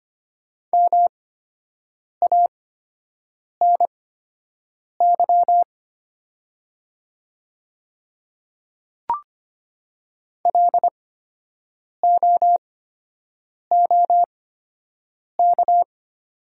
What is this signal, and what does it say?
Morse code, heard as a keyed tone, about 24 wpm, spaced as for 7 wpm, 690 Hz. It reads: MANY LOOK
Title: 8x Character Spacing - Top 200 Words - 25wpm